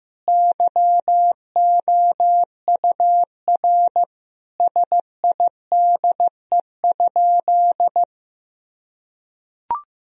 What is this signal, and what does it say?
Morse code, heard as a keyed tone, about 15 wpm, 700 Hz. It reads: YOUR SIDE?
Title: Sentences from Top 1000 Words - Review Part 3 - 15wpm